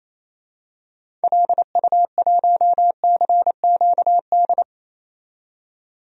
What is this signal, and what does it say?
Morse code, heard as a keyed tone, about 27 wpm, 700 Hz. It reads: LU1CQD
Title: DXCC-100 Call Signs - 6 Characters - 28wpm